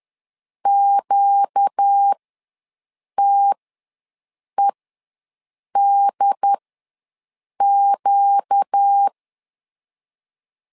Morse code, heard as a keyed tone, about 10 wpm, spaced as for 6 wpm, 800 Hz. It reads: QTEDQ
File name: LEZIONI / 4 Q ws 10wpm 2306